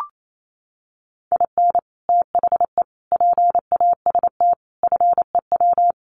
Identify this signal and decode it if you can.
Morse code, heard as a keyed tone, about 28 wpm, 700 Hz. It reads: IN THE PAST FEW
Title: Sets of 5 Words 28wpm